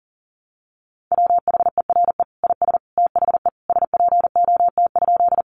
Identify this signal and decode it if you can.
Morse code, heard as a keyed tone, about 40 wpm, 700 Hz. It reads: WHERE IS THE SPOT?
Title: Sentences from Top 1100 Words 40wpm